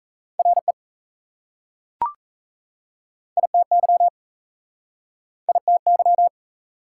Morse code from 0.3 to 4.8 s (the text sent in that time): RE ITY